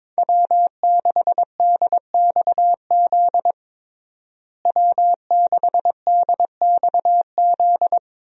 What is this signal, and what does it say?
Morse code, heard as a keyed tone, about 22 wpm, 700 Hz. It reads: W6DXZ W6DXZ
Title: USA Call Signs - 5 Characters - Repeated 2x - 22wpm